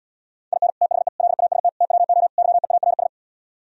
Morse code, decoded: IN THE LAST TWO YEARS

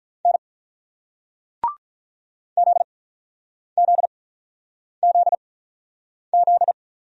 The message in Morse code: N ZZZZ

